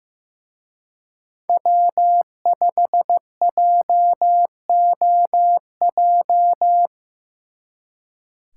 Morse code, heard as a keyed tone, about 18 wpm, 700 Hz. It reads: W5JOJ
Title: Speed Racing - USA 1x3 Call Signs 15wpm